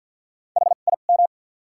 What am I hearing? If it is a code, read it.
Morse code, heard as a keyed tone, about 48 wpm, spaced as for 35 wpm, 700 Hz. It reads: HIM
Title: 2x Character Spacing - Top 300 Words - 50wpm